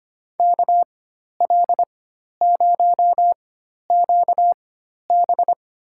Morse code, heard as a keyed tone, about 25 wpm, spaced as for 12 wpm, 700 Hz. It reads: KL0QB
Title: USA Call Signs - 5 Characters - 4x Character Spacing - 25wpm